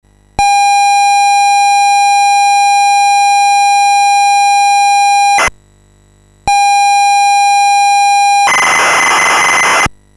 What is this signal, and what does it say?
ZX Spectrum tape loading.